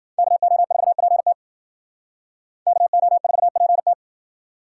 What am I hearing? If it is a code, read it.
Morse code, heard as a keyed tone, about 60 wpm, 700 Hz. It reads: KO4PT KO4PT